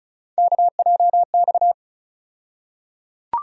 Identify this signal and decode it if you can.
Morse code, heard as a keyed tone, about 36 wpm, 700 Hz. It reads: KJX
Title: USA Call Sign Postfixes - 3 Letters 35wpm